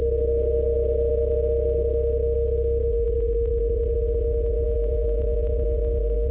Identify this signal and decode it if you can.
PSK31, about 31.25 baud.
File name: luxorion / Radio / qso-bpsk31-g4ydo-s51ab- -1408z-14071-6khz